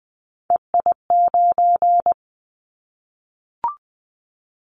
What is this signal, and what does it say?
Morse code, heard as a keyed tone, about 20 wpm, 700 Hz. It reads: EI9
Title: Callsign Prefixes for Top 50 DXCC - 20wpm